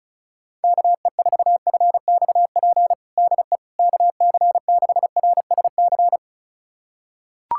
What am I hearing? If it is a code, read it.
Morse code, heard as a keyed tone, about 36 wpm, 700 Hz. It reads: KE4FXP DE KC6RSC
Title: USA Callsign DE Callsign - 6 Characters - 35wpm